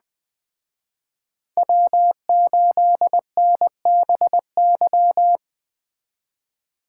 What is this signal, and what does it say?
Morse code, heard as a keyed tone, about 20 wpm, 700 Hz. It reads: W8NBY